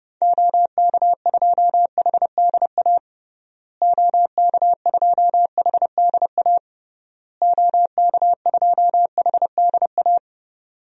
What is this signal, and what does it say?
Morse code, heard as a keyed tone, about 30 wpm, 700 Hz. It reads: OK2HDA OK2HDA OK2HDA
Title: DXCC-100 Call Signs - 6 Characters - Repeated 3x - 30wpm